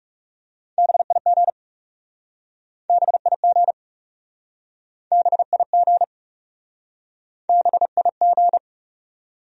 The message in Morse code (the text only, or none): BIG BIG BIG BIG